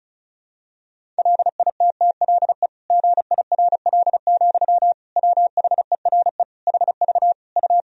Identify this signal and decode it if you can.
Morse code, heard as a keyed tone, about 34 wpm, 700 Hz. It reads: LITTLE GIRL, WHERE HV U
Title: Sentences from Top 400 Words and Abbreviations 35wpm